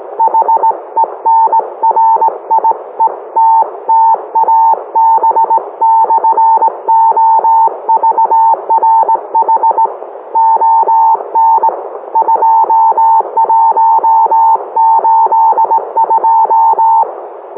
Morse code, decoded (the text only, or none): HENRIETTA B/OVRH ON 2182